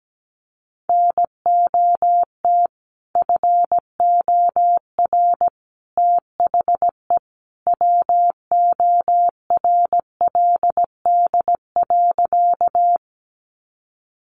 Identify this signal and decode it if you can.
Morse code, heard as a keyed tone, about 17 wpm, 700 Hz. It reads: NOT FOR THE WORLD.
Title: Sentences from Top 600 Words 17wpm